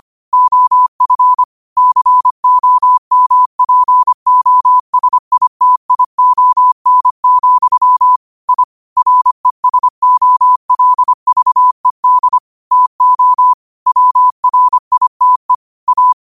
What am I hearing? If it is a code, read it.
Morse code, heard as a keyed tone, about 25 wpm, 1000 Hz. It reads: OF COMPOSITION, I RESOLVED TO WRITE A